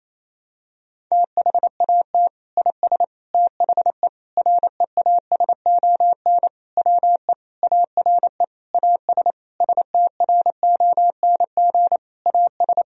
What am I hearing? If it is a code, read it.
Morse code, heard as a keyed tone, about 28 wpm, 700 Hz. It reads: THAT IS THE REASON WE ARE AS STRONG AS
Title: Sentences from Top 1000 Words - Review Part 4 - 28wpm